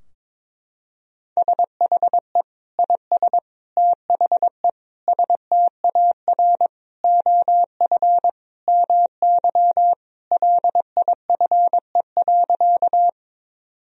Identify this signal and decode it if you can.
Morse code, heard as a keyed tone, about 22 wpm, 700 Hz. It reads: SHE IS THE STAR OF MY LIFE.
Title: Sentences from Top 1000 Words - Review Part 2 - 22wpm